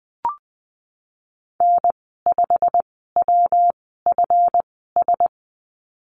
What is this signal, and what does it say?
Morse code, heard as a keyed tone, about 20 wpm, spaced as for 15 wpm, 700 Hz. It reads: N5WFS
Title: 147 Lesson 24g - Letter y - Call Signs 20wpm